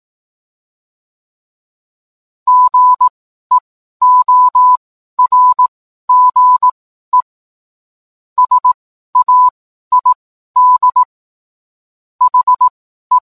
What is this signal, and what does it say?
Morse code, heard as a keyed tone, about 18 wpm, spaced as for 12 wpm, 1000 Hz. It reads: GEORGE SAID HE